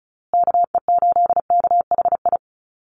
Morse code, heard as a keyed tone, about 34 wpm, 700 Hz. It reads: KE8KHI
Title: USA Call Signs Repeated 2x 35wpm